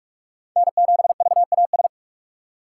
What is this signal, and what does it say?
Morse code, heard as a keyed tone, about 48 wpm, 700 Hz. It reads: N7VAS
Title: USA Call Signs - 2x Character Spacing - 45wpm